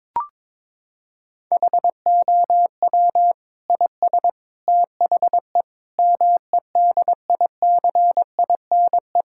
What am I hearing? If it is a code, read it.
Morse code, heard as a keyed tone, about 21 wpm, 700 Hz. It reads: HOW IS THE MEDICINE
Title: Sentences from Top 1000 Words - Review Part 6 - 22wpm